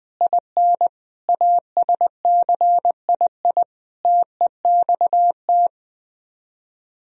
Morse code, heard as a keyed tone, about 20 wpm, 700 Hz. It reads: IN ASCII TEXT
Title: Speed-Racing - Sets of 5 Words 17wpm